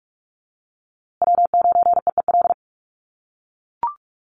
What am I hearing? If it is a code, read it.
Morse code, heard as a keyed tone, about 44 wpm, 700 Hz. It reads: W9EEL
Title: USA Call Signs Repeated 2x 45wpm